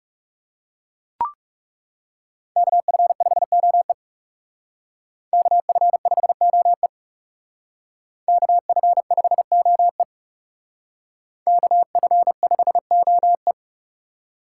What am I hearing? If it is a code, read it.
Morse code, heard as a keyed tone, about 35 wpm, 700 Hz. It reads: KF5OE KF5OE KF5OE KF5OE
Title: Speed Racing - USA 2x2 Call Signs 30wpm